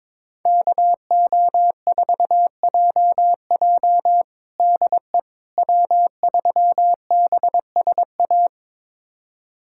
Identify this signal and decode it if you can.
Morse code, heard as a keyed tone, about 22 wpm, 700 Hz. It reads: KO4JJ DE W3BSA
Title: USA Callsign DE Callsign - 5 Characters - 22wpm